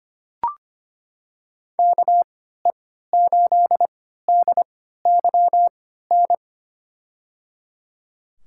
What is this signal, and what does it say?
Morse code, heard as a keyed tone, about 25 wpm, spaced as for 14 wpm, 700 Hz. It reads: KE8DYN
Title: USA Call Signs - 3x Character Spacing - 25wpm